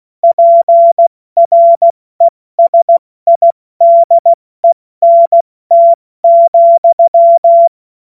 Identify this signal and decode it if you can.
Morse code, heard as a keyed tone, about 16 wpm, 680 Hz. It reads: PRESIDENT,